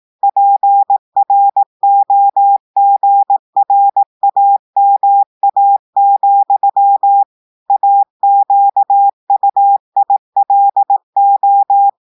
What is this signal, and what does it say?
Morse code, heard as a keyed tone, about 18 wpm, 800 Hz. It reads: PROGRAMA, AQUILO
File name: audios / portugais / portugais texte en clair 2 vitesse 18 mots minutes